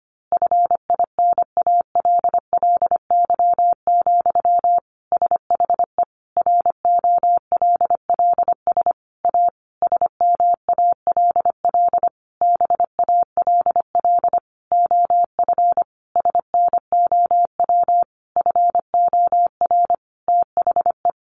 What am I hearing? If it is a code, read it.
Morse code, heard as a keyed tone, about 25 wpm, 700 Hz. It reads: FINALLY, SHE ROLLS A SMALL BALL OF SNOW FOR THE
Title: Easy Phrases 25wpm